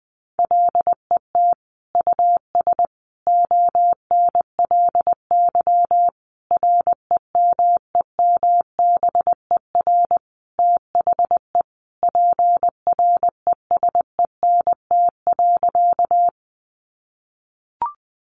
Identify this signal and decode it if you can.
Morse code, heard as a keyed tone, about 20 wpm, 700 Hz. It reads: LET US ONLY REMEMBER THE PRESENT.
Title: Sentences from Top 800 Words 20wpm